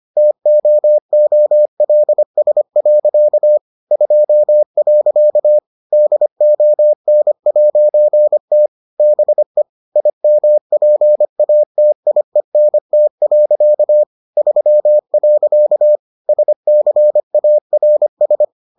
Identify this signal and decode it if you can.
Morse code, heard as a keyed tone, about 25 wpm, 600 Hz. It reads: TOOLS. 2. DON'T BE IMPATIENT. 3. SCARS